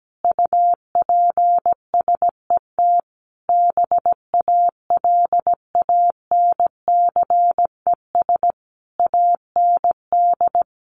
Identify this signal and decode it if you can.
Morse code, heard as a keyed tone, about 17 wpm, 700 Hz. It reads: UPSET BALANCES AND
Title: A Sound Of Thunder 17wpm